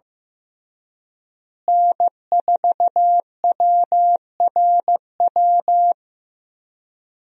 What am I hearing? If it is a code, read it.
Morse code, heard as a keyed tone, about 15 wpm, 700 Hz. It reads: N4WRW
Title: USA Call Signs - 5 Characters - Repeated 3x - 15wpm